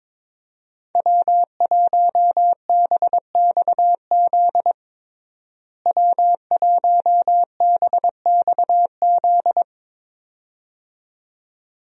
Morse code, heard as a keyed tone, about 22 wpm, 700 Hz. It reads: W1BXZ W1BXZ
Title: USA Call Signs - 5 Characters - Repeated 2x - 22wpm